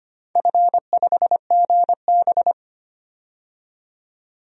Morse code, heard as a keyed tone, about 25 wpm, 700 Hz. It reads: F5GB
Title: DXCC-100 Call Signs - 4 Characters - 25wpm